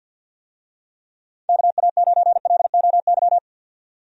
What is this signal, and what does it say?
Morse code, heard as a keyed tone, about 52 wpm, 700 Hz. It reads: KA9LOY
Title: USA Call Signs - 6 Characters - 50wpm